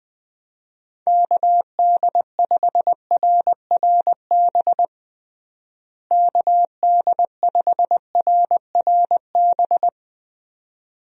Morse code, heard as a keyed tone, about 21 wpm, 700 Hz. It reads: KD5RRB KD5RRB
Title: USA Call Signs - 6 Characters - Repeated 2x - 20wpm